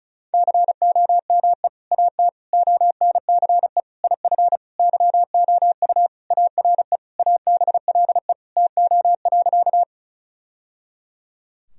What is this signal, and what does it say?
Morse code, heard as a keyed tone, about 36 wpm, 700 Hz. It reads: COME AT ONCE IF YOU ARE ABLE TO.
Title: Sentences from Top 1000 Words - Review Part 2 - 35wpm